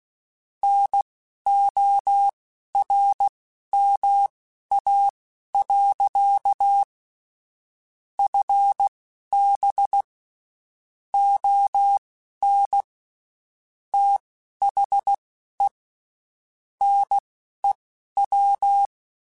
Morse code, heard as a keyed tone, about 16 wpm, spaced as for 11 wpm, 800 Hz. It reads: NORMA. FB ON THE NEW
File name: (spaced) / QSOs Spaced-mp3s / WD0FBT 1 @15 sp